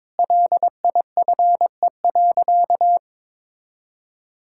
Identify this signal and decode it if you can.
Morse code, heard as a keyed tone, about 22 wpm, 700 Hz. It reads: LIFE.
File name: practice / 22wpm / Sentences with 8-Word Phrases - 22wpm